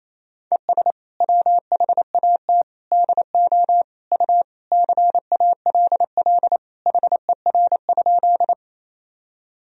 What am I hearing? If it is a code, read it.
Morse code, heard as a keyed tone, about 28 wpm, 700 Hz. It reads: ES WHAT DO U CALL HER?
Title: Sentences from Top 100 Words and Abbreviations 28wpm